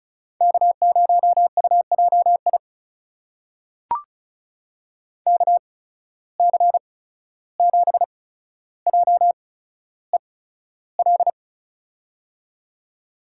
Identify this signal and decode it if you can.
Morse code, heard as a keyed tone, about 36 wpm, spaced as for 14 wpm, 700 Hz. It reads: K0UJI KC7JEL
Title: USA Call Signs - 8x Character Spacing - 35wpm